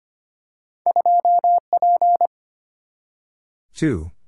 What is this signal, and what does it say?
Morse code, heard as a keyed tone, about 25 wpm, 700 Hz. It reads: 2P
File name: Two Letter-Number Combinations - 25wpm